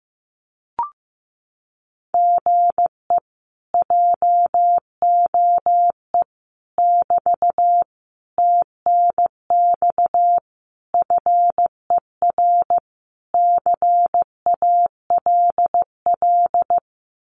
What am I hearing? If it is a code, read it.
Morse code, heard as a keyed tone, about 16 wpm, 700 Hz. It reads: GE JOE <BT> TNX FER CALL
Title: Common QSO Phrases Encore 15wpm